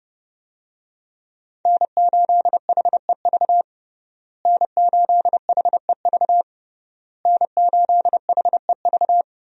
Morse code, heard as a keyed tone, about 29 wpm, 700 Hz. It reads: N8HEV N8HEV N8HEV
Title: USA Call Signs - 5 Characters - Repeated 3x - 30wpm